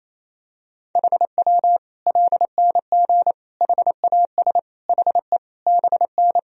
Morse code, heard as a keyed tone, about 28 wpm, 700 Hz. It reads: HW LNG HAS HE BN
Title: Sentences from Top 100 Words and Abbreviations 28wpm